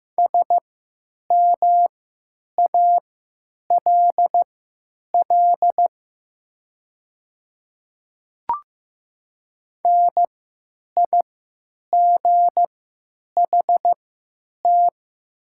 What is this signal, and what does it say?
Morse code, heard as a keyed tone, about 15 wpm, spaced as for 9 wpm, 700 Hz. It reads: SMALL NIGHT
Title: 3x Character Spacing - Top 300 Words - 15wpm